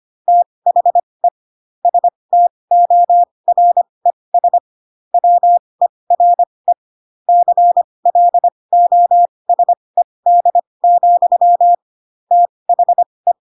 Morse code, heard as a keyed tone, about 25 wpm, spaced as for 20 wpm, 700 Hz. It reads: THE STORES WERE CLOSED, THE
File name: Pinocchio / Pinocchio ch6 15 25 ik1hge